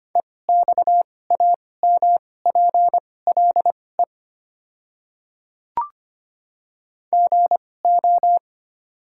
Morse code, heard as a keyed tone, about 25 wpm, spaced as for 18 wpm, 690 Hz. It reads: EXAMPLE GO
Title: 2x Character Spacing - Top 300 Words - 25wpm